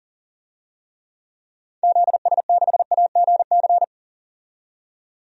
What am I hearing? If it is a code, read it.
Morse code, heard as a keyed tone, about 39 wpm, 700 Hz. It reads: ZS6AGC
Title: DXCC-100 Call Signs - 6 Characters - 40wpm